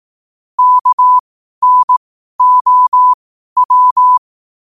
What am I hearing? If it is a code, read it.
Morse code, heard as a keyed tone, about 18 wpm, spaced as for 12 wpm, 1000 Hz. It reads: KNOW
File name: qotdfiles / 12wpm